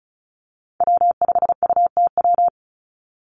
Morse code, heard as a keyed tone, about 34 wpm, 700 Hz. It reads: W5UTW
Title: Call Signs - 35wpm